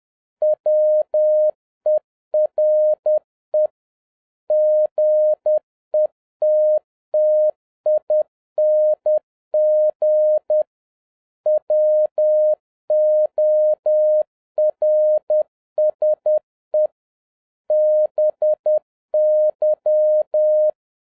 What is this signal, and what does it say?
Morse code, heard as a keyed tone, about 10 wpm, 610 Hz. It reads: WERE GETTING WORSE BY
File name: 2011-07-mp3 / n36 THE 3.11 2 10wpm